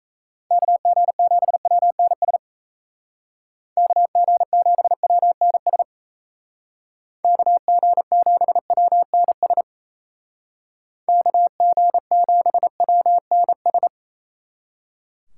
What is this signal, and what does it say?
Morse code, heard as a keyed tone, about 34 wpm, 700 Hz. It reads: KG7WNS KG7WNS KG7WNS KG7WNS
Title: Speed Racing - USA 2x3 Call Signs 28wpm